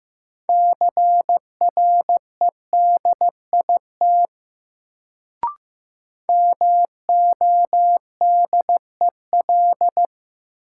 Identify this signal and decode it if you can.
Morse code, heard as a keyed tone, about 15 wpm, 700 Hz. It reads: CREDIT MODEL
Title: Sentences from Top 1000 Words - New Word Drill - 15wpm